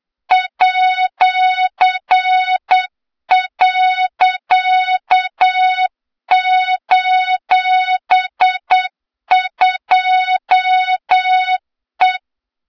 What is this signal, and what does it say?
Morse code, heard as a keyed tone, about 8 wpm, 3700 Hz, with noise. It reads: @.:2E
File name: Morze-HA2MN / hang / 23 GYAKORLAT